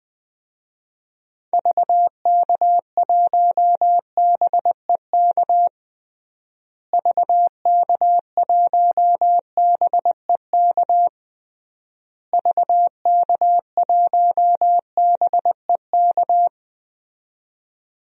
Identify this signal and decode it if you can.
Morse code, heard as a keyed tone, about 20 wpm, 700 Hz. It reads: VK1BEK VK1BEK VK1BEK
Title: DXCC-100 Call Signs - 6 Characters - Repeated 3x - 20wpm